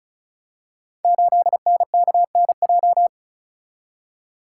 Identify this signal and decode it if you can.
Morse code, heard as a keyed tone, about 36 wpm, 700 Hz. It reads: 8NKNJ
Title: Code Groups - 5 Letter:Numbers - 35wpm